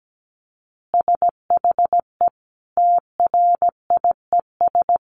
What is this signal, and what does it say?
Morse code, heard as a keyed tone, about 17 wpm, 700 Hz. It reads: SHE TRIES
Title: Easy Phrases 17wpm